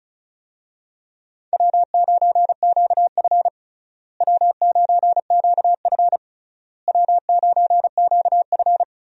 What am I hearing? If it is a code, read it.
Morse code, heard as a keyed tone, about 36 wpm, 700 Hz. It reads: W9QF W9QF W9QF
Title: USA Call Signs - 4 Characters - Repeated 3x - 35wpm